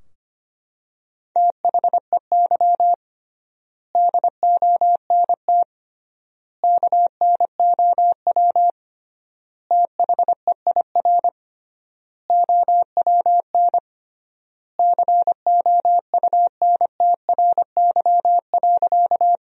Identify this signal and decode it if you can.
Morse code, heard as a keyed tone, about 25 wpm, 700 Hz. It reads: THEY DONT KNOW THEIR OWN COUNTRY.
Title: 3x Word-Spacing - Sentences from Top 300 Words - 25wpm